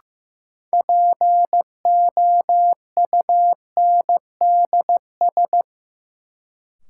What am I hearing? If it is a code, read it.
Morse code, heard as a keyed tone, about 15 wpm, 700 Hz. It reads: POUNDS
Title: Sentences from Top 1000 Words - Review Part 2 - New Word Drill - 15wpm